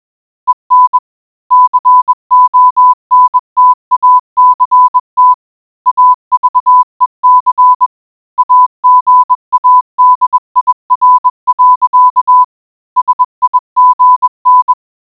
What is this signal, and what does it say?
Morse code, heard as a keyed tone, about 20 wpm, 1000 Hz. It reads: EN CONTACT AVEC AGADIR. SIGN